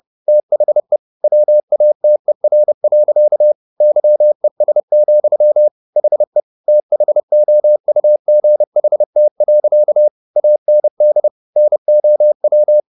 Morse code, heard as a keyed tone, about 30 wpm, 600 Hz. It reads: THE WATER. YES, HE THOUGHT. AND NOW